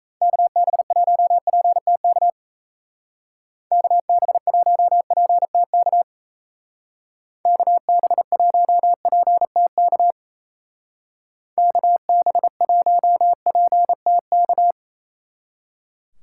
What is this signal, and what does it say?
Morse code, heard as a keyed tone, about 35 wpm, 700 Hz. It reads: KB1PTK KB1PTK KB1PTK KB1PTK
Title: Speed Racing - USA 2x3 Call Signs 28wpm